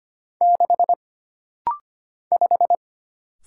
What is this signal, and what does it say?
Morse code, heard as a keyed tone, about 25 wpm, 700 Hz. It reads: 6 5
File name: Single Character (Letter:Number:Punctuation:Prosign) - Rapid-Fire - 25wpm